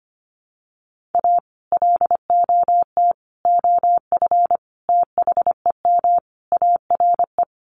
Morse code, heard as a keyed tone, about 25 wpm, 700 Hz. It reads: A LOT OF THEM ARE
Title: Sets of 5 Words Encore2 25wpm